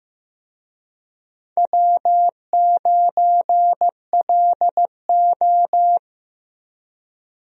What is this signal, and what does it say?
Morse code, heard as a keyed tone, about 15 wpm, 700 Hz. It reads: W9LO
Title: USA Call Signs - 4 Characters - Repeated 2x - 15wpm